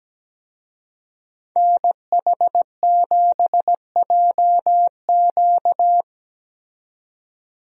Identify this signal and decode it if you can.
Morse code, heard as a keyed tone, about 17 wpm, 700 Hz. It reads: NH7JQ
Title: USA Call Signs - 5 Characters - 17wpm